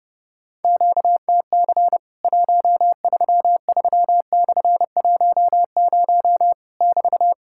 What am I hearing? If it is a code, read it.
Morse code, heard as a keyed tone, about 31 wpm, 700 Hz. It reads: QTC 133/10 <BT>
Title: Worked All Europe DX Contest - QTC Exchange - 30wpm